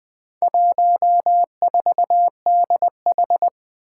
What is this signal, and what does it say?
Morse code, heard as a keyed tone, about 20 wpm, 700 Hz. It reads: 14DH
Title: Four Challenging Letter-Number Combinations 20wpm